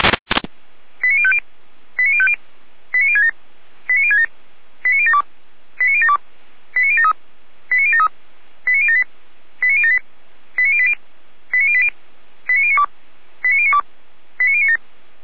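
Five-tone selective call (ZVEI), heard as quick runs of five tones, tones of about 70 ms. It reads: 89050 89050 89067 89067 89072 89072 89073 89073 89078 89078 89080 89080 89092 89092 89097
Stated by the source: ZVEI five-tone call received by SDR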